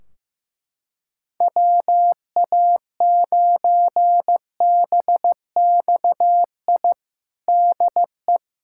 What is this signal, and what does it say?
Morse code, heard as a keyed tone, about 15 wpm, 700 Hz. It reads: WA9BXI DE
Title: Practice QSO - Exchange 4 - Ending - 15wpm